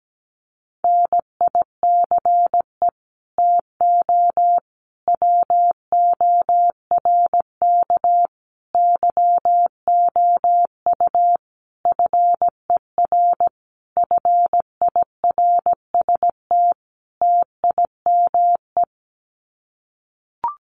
Morse code, heard as a keyed tone, about 17 wpm, 700 Hz. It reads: NICE TO WORK YOU FER FIRST TIME
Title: Common QSO Phrases 17wpm